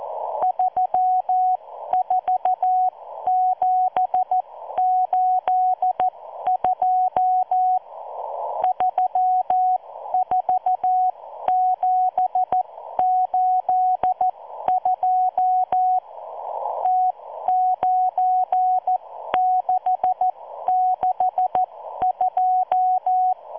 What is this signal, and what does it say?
Morse code, heard as a keyed tone, about 14 wpm, spaced as for 12 wpm, 740 Hz. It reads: 34782 34782 T9662